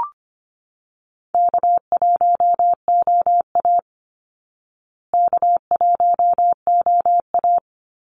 Morse code, heard as a keyed tone, about 25 wpm, 700 Hz. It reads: K1OA K1OA
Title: Call Signs Repeated 2x - 25wpm